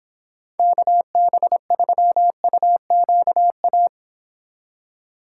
Morse code, heard as a keyed tone, about 26 wpm, 700 Hz. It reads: KB3UQA